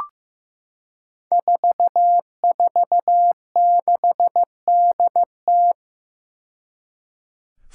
Morse code, heard as a keyed tone, about 15 wpm, 700 Hz. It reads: 446DT
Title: Five Challenging Letter-Number Combinations 15wpm